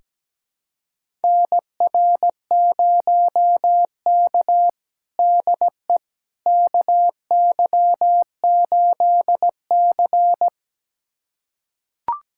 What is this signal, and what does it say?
Morse code, heard as a keyed tone, about 17 wpm, 700 Hz. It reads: NR0K DE KY8C
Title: USA Callsign DE Callsign - 4 Characters - 17wpm